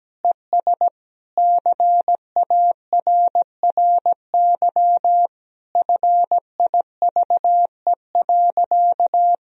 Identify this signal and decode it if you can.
Morse code, heard as a keyed tone, about 17 wpm, 700 Hz. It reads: ES CARRY FIVE.
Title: Sentences from Top 400 Words and Abbreviations 17wpm